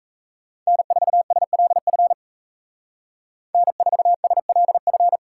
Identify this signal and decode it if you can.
Morse code, heard as a keyed tone, about 36 wpm, 700 Hz. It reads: N4SLF N4SLF